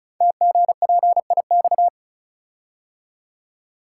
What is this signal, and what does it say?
Morse code, heard as a keyed tone, about 36 wpm, 690 Hz. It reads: TGPIX
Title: Code Groups - 5 Letters - 35wpm